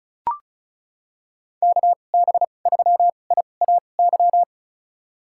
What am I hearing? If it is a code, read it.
Morse code, heard as a keyed tone, about 36 wpm, spaced as for 26 wpm, 690 Hz. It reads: KB3IAY
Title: USA Call Signs - 6 Characters - 2x Character Spacing - 35wpm